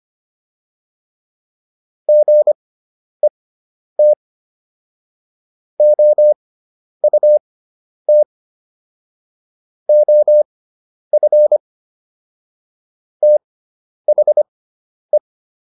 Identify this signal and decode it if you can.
Morse code, heard as a keyed tone, about 25 wpm, spaced as for 10 wpm, 600 Hz. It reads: GET OUT OF THE